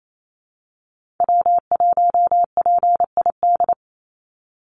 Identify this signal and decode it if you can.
Morse code, heard as a keyed tone, about 28 wpm, 700 Hz. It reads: W1PID
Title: Call Signs Repeated 3x - 28wpm